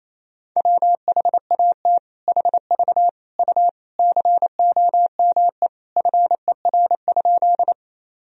Morse code, heard as a keyed tone, about 28 wpm, 700 Hz. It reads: WHAT HV U COME FER?
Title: Sentences from Top 100 Words and Abbreviations 28wpm